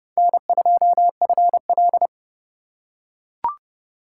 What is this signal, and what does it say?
Morse code, heard as a keyed tone, about 28 wpm, 700 Hz. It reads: N2FL